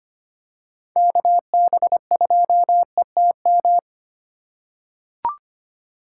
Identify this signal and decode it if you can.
Morse code, heard as a keyed tone, about 25 wpm, 700 Hz. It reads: KB2ETM
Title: USA Call Signs - 3x Character Spacing - 25wpm